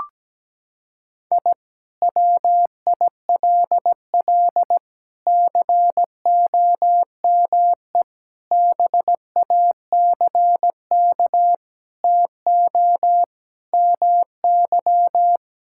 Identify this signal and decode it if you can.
Morse code, heard as a keyed tone, about 17 wpm, 700 Hz. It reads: I WILL COME BACK TO MY
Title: Sentences from Top 1000 Words - Review Part 6 - 17wpm